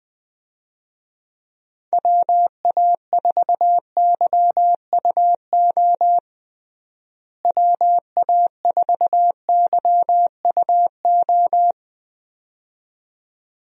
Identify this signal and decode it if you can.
Morse code, heard as a keyed tone, about 21 wpm, 700 Hz. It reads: WA4YUO WA4YUO
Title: USA Call Signs - 6 Characters - Repeated 2x - 20wpm